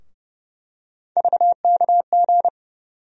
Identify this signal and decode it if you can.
Morse code, heard as a keyed tone, about 30 wpm, 700 Hz. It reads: VKG
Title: USA Call Sign Postfixes - 3 Letters 30wpm